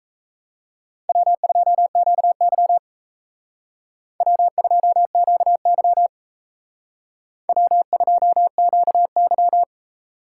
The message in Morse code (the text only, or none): W2QY W2QY W2QY